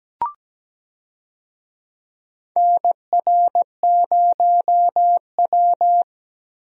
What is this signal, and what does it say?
Morse code, heard as a keyed tone, about 17 wpm, 700 Hz. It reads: NR0W
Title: USA Call Signs - 4 Characters - Repeated 2x - 17wpm